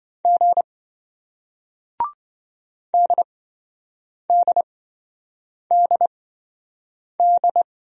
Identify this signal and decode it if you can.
Morse code, heard as a keyed tone, about 27 wpm, spaced as for 7 wpm, 700 Hz. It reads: G DDDD